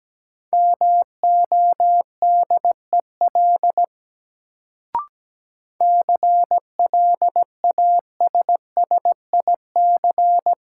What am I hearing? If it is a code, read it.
Morse code, heard as a keyed tone, about 17 wpm, 700 Hz. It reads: MODEL CLASSIC
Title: Sentences from Top 1000 Words - New Word Drill - 17wpm